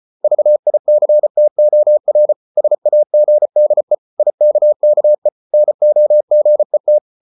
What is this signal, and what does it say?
Morse code, heard as a keyed tone, about 34 wpm, 600 Hz. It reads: VICTOR SAGDE IKKE NOGET
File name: Ferie QRQ 34wpm